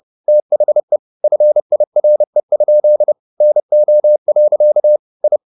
Morse code, heard as a keyed tone, about 30 wpm, 600 Hz. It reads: THE FIRE? NO. I
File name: the old man and the sea / the old man and the sea 1 30wpm